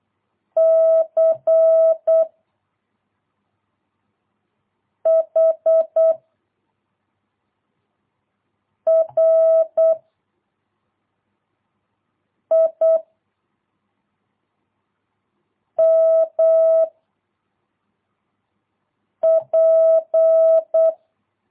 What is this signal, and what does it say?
Morse code, heard as a keyed tone, about 8 wpm, 660 Hz. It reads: C H R I M P
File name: jh8cbh / 2022cw